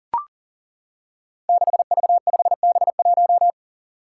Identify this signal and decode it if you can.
Morse code, heard as a keyed tone, about 42 wpm, 700 Hz. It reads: 6V5B1
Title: Five Challenging Letter-Number Combinations 40wpm